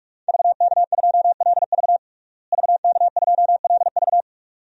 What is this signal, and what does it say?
Morse code, heard as a keyed tone, about 46 wpm, 700 Hz. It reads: VK2LV VK2LV
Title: DXCC-100 Call Signs - 5 Characters - Repeated 2x - 45wpm